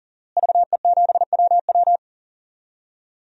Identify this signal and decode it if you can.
Morse code, heard as a keyed tone, about 40 wpm, 700 Hz. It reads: VE7WW
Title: DXCC-100 Call Signs - 5 Characters - Repeated 3x - 40wpm